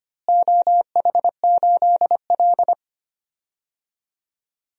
Morse code, heard as a keyed tone, about 25 wpm, 700 Hz. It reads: OH8L
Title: DXCC-100 Call Signs - 4 Characters - Repeated 3x - 25wpm